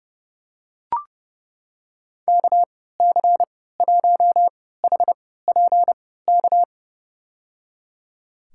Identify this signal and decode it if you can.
Morse code, heard as a keyed tone, about 30 wpm, spaced as for 17 wpm, 700 Hz. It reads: KC1HPK
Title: USA Call Signs - 6 Characters - 3x Character Spacing - 30wpm